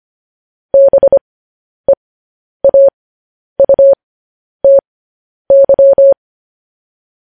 Morse code, heard as a keyed tone, about 25 wpm, spaced as for 10 wpm, 550 Hz. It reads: BEAUTY